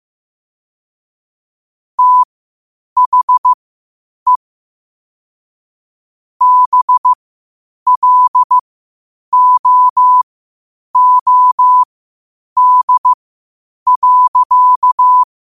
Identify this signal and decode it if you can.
Morse code, heard as a keyed tone, about 15 wpm, spaced as for 8 wpm, 1000 Hz. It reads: THE BLOOD.
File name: qotdfiles / 8wpm